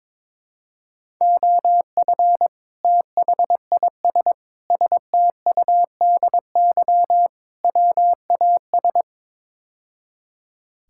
Morse code, heard as a keyed tone, about 21 wpm, 700 Hz. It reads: OF THIS STUDY WAS
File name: practice / 22wpm / Build A Phrase - 5-Word Phrases - 22wpm